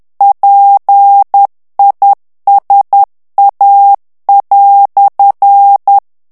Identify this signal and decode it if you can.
Morse code, heard as a keyed tone, about 10 wpm, 800 Hz. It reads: PISA"
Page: Français - texte en clair 1 - vitesse : 10 mots minute